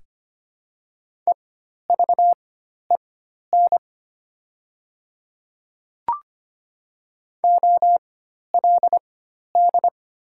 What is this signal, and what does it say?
Morse code, heard as a keyed tone, about 25 wpm, spaced as for 12 wpm, 700 Hz. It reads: EVEN OLD
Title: 4x Character Spacing - Top 300 Words - 25wpm